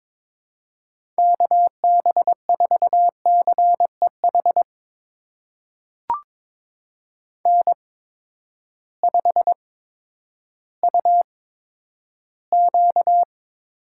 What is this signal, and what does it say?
Morse code, heard as a keyed tone, about 22 wpm, spaced as for 11 wpm, 700 Hz. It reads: KB4CEH N5UQ
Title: USA Call Signs - 8x Character Spacing - 22wpm